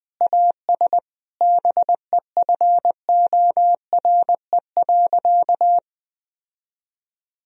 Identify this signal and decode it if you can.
Morse code, heard as a keyed tone, about 21 wpm, 700 Hz. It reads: AS BEFORE.
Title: Sentences from Top 1000 Words - Review Part 2 - 20wpm